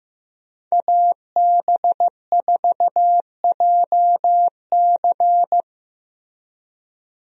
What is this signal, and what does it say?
Morse code, heard as a keyed tone, about 15 wpm, 700 Hz. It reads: AB4JC
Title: USA Call Signs - 5 Characters - 15wpm